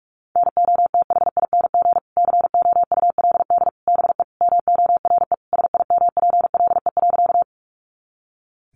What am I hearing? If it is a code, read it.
Morse code, heard as a keyed tone, about 44 wpm, 700 Hz. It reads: NOTHING COULD BE MORE SIMPLE.
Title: Sentences from Top 500 Words 45wpm